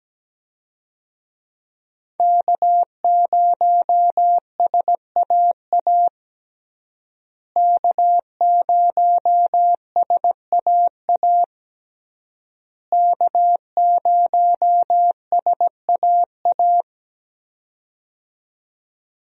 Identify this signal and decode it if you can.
Morse code, heard as a keyed tone, about 17 wpm, 700 Hz. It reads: K0SAA K0SAA K0SAA
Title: USA Call Signs - 5 Characters - Repeated 3x - 17wpm